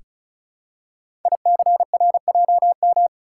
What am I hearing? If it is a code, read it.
Morse code, heard as a keyed tone, about 36 wpm, 690 Hz. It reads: ICRJM
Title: Code Groups - 5 Letters - 35wpm